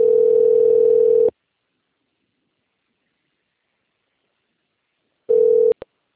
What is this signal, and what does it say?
Ringback tone (North America), 440 + 480 Hz.
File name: out- - - - - .89317